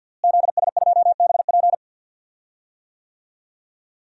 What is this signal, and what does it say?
Morse code, heard as a keyed tone, about 53 wpm, 700 Hz. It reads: ZS2BP
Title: DXCC-100 Call Signs - 5 Characters - Repeated 2x - 50wpm